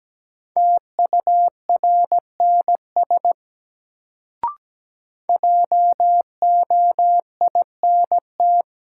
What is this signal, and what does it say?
Morse code, heard as a keyed tone, about 17 wpm, 700 Hz. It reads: TURNS JOINT
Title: Sets of 5 Words Encore - New Word Drill - 17wpm